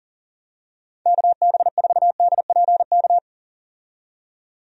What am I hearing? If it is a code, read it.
Morse code, heard as a keyed tone, about 40 wpm, 700 Hz. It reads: KB4DPK
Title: USA Call Signs - 6 Characters - 40wpm